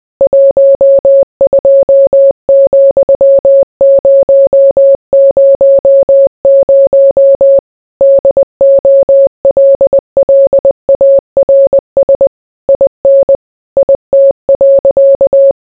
Morse code, heard as a keyed tone, about 20 wpm, 550 Hz. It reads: 12,000 DOLLARS IN IT.